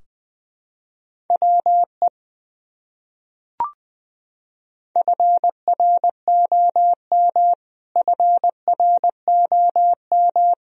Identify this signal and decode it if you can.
Morse code, heard as a keyed tone, about 21 wpm, 700 Hz. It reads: WE FROM FROM
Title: Top 100 Words Repeated 20wpm